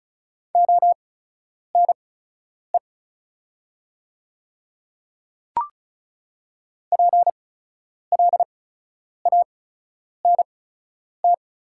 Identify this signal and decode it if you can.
Morse code, heard as a keyed tone, about 36 wpm, spaced as for 10 wpm, 700 Hz. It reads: ONE PLANT
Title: 8x Character Spacing - Top 300 Words - 35wpm